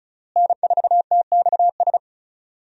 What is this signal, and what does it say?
Morse code, heard as a keyed tone, about 36 wpm, 700 Hz. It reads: N4TXS
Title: USA Call Signs - 5 Characters - Repeated 3x - 35wpm